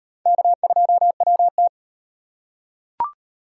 Morse code, heard as a keyed tone, about 38 wpm, 700 Hz. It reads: K2WT